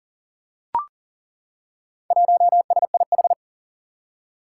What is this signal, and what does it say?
Morse code, heard as a keyed tone, about 41 wpm, 700 Hz. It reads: 1SIH
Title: Four Challenging Letter-Number Combinations 40wpm